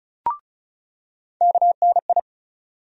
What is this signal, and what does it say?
Morse code, heard as a keyed tone, about 34 wpm, 700 Hz. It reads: KNI
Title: USA Call Sign Postfixes - 3 Letters 35wpm